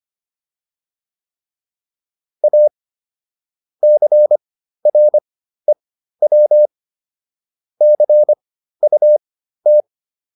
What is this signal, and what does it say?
Morse code, heard as a keyed tone, about 25 wpm, spaced as for 13 wpm, 600 Hz. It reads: A CREW CUT